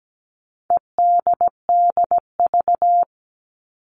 Morse code, heard as a keyed tone, about 17 wpm, 700 Hz. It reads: EDDV
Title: All Cut Numbers - 4-Digit 17wpm